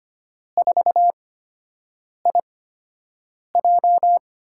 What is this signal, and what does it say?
Morse code, heard as a keyed tone, about 25 wpm, spaced as for 7 wpm, 700 Hz. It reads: 4IJ
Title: USA Call Signs - 5 Characters - 8x Character Spacing - 25wpm